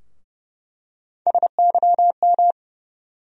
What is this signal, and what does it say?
Morse code, heard as a keyed tone, about 31 wpm, 700 Hz. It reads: SYM
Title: USA Call Sign Postfixes - 3 Letters 30wpm